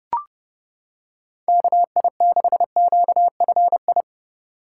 Morse code, heard as a keyed tone, about 30 wpm, 700 Hz. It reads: KI6QFI